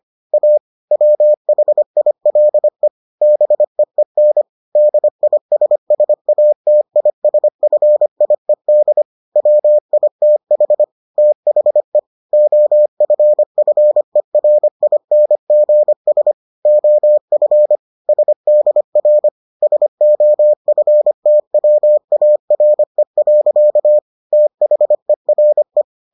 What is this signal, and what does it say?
Morse code, heard as a keyed tone, about 25 wpm, 600 Hz. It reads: A WHILE BEEN DISSATISFIED WITH THE OFFERINGS OF SDR SOFTWARE. THERE